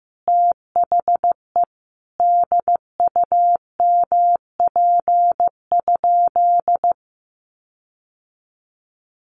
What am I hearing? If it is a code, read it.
Morse code, heard as a keyed tone, about 15 wpm, 700 Hz. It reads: THE DUMP?
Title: Sentences from Top 1300 Words 15wpm